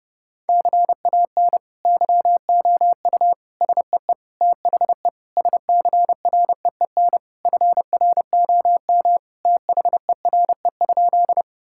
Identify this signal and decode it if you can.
Morse code, heard as a keyed tone, about 32 wpm, 700 Hz. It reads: CAN YOU SEE THE SCREEN FROM THERE?
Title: Sentences from Top 1000 Words - Review Part 6 - 30wpm